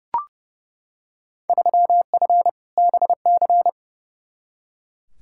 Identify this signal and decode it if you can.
Morse code, heard as a keyed tone, about 30 wpm, 700 Hz. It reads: 3F BC
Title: Field Day Exchanges Encore - 30wpm